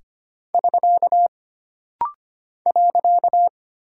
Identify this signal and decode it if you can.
Morse code, heard as a keyed tone, about 25 wpm, 700 Hz. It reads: <SK> .
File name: Single Punctuation-Prosign - Rapid-Fire - 25wpm